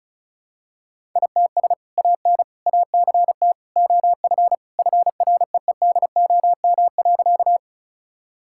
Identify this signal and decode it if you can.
Morse code, heard as a keyed tone, about 36 wpm, 700 Hz. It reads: ITS AN ACT OF FREEDOM.
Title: Sentences from Top 1000 Words - Review Part 5 - 35wpm